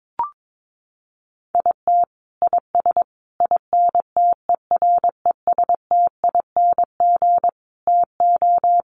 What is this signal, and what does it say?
Morse code, heard as a keyed tone, about 22 wpm, 700 Hz. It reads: IT IS INTERESTING TO
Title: Sets of 5 Words Encore2 22wpm